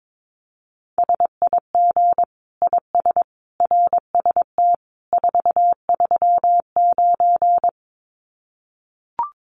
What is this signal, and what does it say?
Morse code, heard as a keyed tone, about 22 wpm, 700 Hz. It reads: SIG IS RST 439
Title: Common QSO Phrases 22wpm